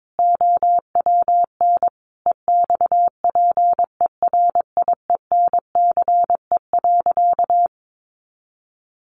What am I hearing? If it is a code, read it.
Morse code, heard as a keyed tone, about 22 wpm, 700 Hz. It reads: OWN EXPERIENCE.
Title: Sentences from Top 600 Words 22wpm